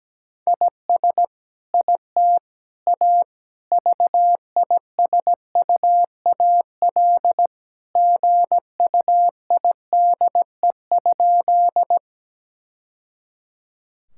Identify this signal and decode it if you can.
Morse code, heard as a keyed tone, about 17 wpm, 700 Hz. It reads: IS IT A VISUAL GUIDE?
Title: Sentences from Top 1000 Words - Review Part 6 - 17wpm